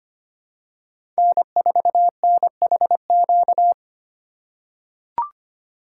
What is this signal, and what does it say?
Morse code, heard as a keyed tone, about 25 wpm, 700 Hz. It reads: N4NHQ
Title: USA Call Signs - 5 Characters - 2x Character Spacing - 25wpm